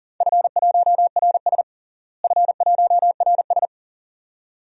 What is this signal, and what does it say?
Morse code, heard as a keyed tone, about 39 wpm, 690 Hz. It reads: F1RS F1RS
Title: DXCC-100 Call Signs - 4 Characters - Repeated 2x - 40wpm